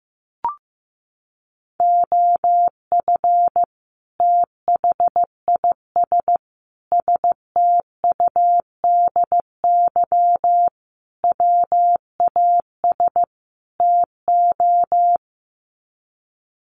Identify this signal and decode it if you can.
Morse code, heard as a keyed tone, about 15 wpm, 700 Hz. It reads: OF THIS STUDY WAS TO
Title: Sets of 5 Words 15wpm